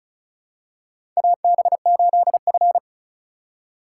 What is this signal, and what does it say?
Morse code, heard as a keyed tone, about 36 wpm, 700 Hz. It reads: AB8F
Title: USA Call Signs - 4 Characters - 4x Character Spacing - 35wpm